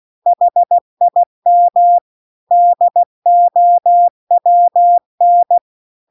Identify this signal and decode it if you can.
Morse code, heard as a keyed tone, about 16 wpm, 700 Hz. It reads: HIM DOWN